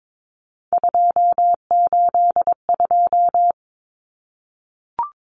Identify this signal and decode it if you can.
Morse code, heard as a keyed tone, about 22 wpm, 700 Hz. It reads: 282
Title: Numbers - 3-Digit 22wpm